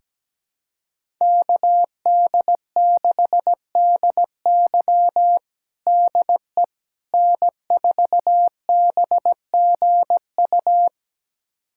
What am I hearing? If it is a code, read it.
Morse code, heard as a keyed tone, about 17 wpm, 700 Hz. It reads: KD6DY DE N4BGU
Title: USA Callsign DE Callsign - 5 Characters - 17wpm